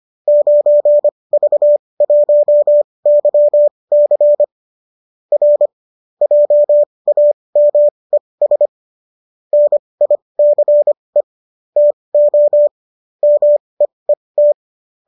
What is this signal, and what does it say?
Morse code, heard as a keyed tone, about 25 wpm, spaced as for 20 wpm, 590 Hz. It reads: 9V1YC R JAMES NICE TO MEET